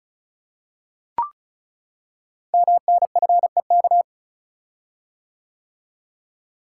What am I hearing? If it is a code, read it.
Morse code, heard as a keyed tone, about 36 wpm, 690 Hz. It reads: MNFEK
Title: Code Groups - 5 Letters - 35wpm